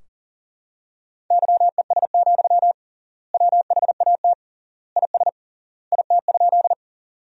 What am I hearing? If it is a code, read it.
Morse code, heard as a keyed tone, about 40 wpm, 700 Hz. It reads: YES, WHAT IS IT?
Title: 3x Word-Spacing - Sentences from Top 500 Words - 40wpm